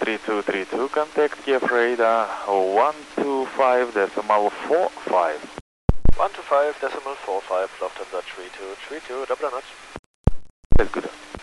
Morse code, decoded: none

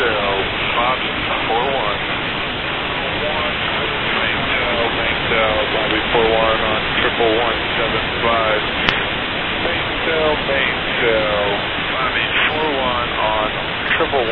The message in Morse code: none

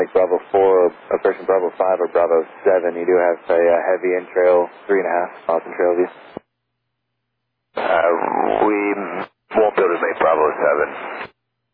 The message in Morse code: none

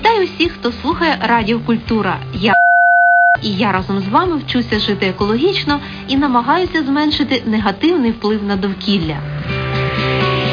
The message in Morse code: none

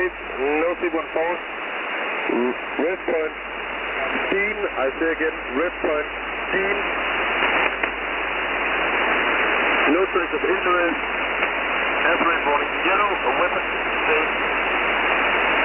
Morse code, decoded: none

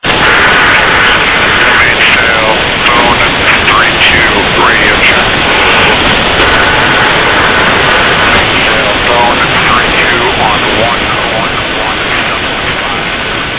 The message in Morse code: none

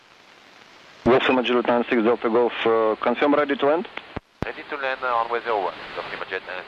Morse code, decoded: none

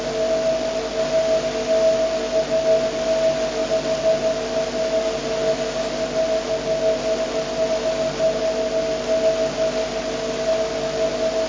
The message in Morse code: none

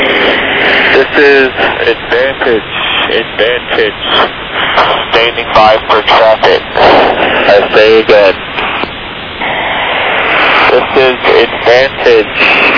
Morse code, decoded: none

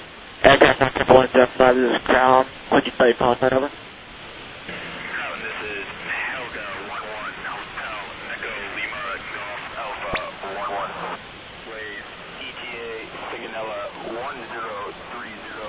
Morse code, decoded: none

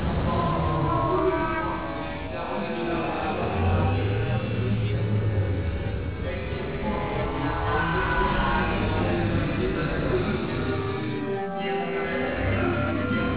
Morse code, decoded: none

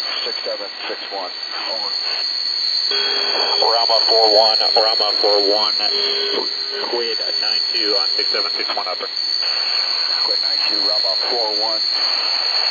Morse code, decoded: none

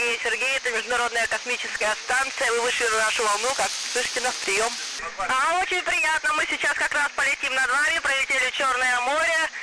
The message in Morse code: none

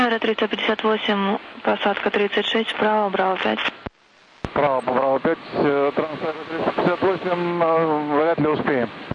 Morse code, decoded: none

